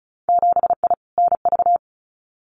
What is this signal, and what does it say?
Morse code, heard as a keyed tone, about 36 wpm, 700 Hz. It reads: 7I NV
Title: Winter Field Day Exchanges Encore 35wpm